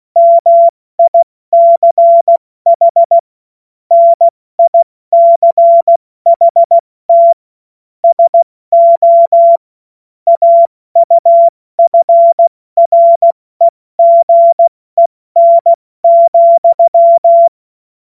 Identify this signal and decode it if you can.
Morse code, heard as a keyed tone, about 16 wpm, 680 Hz. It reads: MICH NICHT SO AUFREGEN,
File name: Morsekurs / Witze / Witz 20. Juni 2023 70 F80